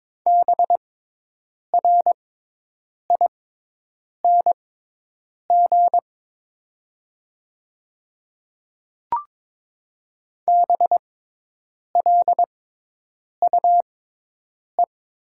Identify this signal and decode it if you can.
Morse code, heard as a keyed tone, about 22 wpm, spaced as for 8 wpm, 700 Hz. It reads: BRING BLUE
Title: 6x Character Spacing - Top 500 Words - 22wpm